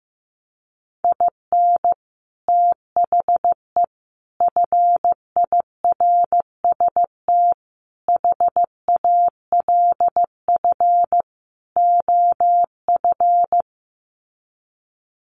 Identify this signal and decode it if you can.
Morse code, heard as a keyed tone, about 15 wpm, 700 Hz. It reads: IN THE FIRST HALF OF
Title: Sets of 5 Words 15wpm